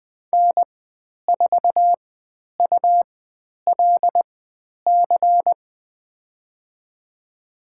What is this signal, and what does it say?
Morse code, heard as a keyed tone, about 20 wpm, spaced as for 10 wpm, 700 Hz. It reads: N4ULC
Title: 102 Lesson 17g - Letter u - Call Signs 20wpm